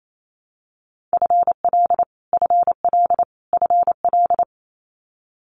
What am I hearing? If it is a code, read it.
Morse code, heard as a keyed tone, about 28 wpm, 700 Hz. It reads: FL FL FL
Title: Multi-letter Phonemes Repeated 3x 28wpm